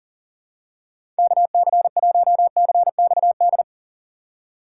Morse code, heard as a keyed tone, about 40 wpm, 700 Hz. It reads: KC1CXD
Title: USA Call Signs - 6 Characters - 2x Character Spacing - 40wpm